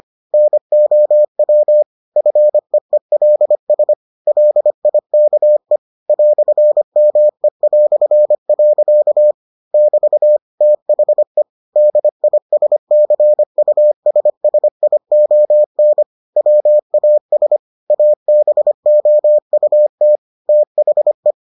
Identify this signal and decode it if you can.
Morse code, heard as a keyed tone, about 25 wpm, 600 Hz. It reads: NOW FEELS LIKE "ME". <BT> THE DISCUSSION WAS ABOUT THE